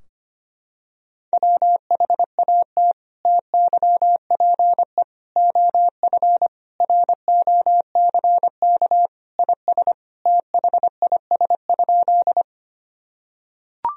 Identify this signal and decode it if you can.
Morse code, heard as a keyed tone, about 25 wpm, 700 Hz. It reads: WHAT TYPE OF ROCK IS THIS?
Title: Sentences from Top 1000 Words - Review Part 3 - 25wpm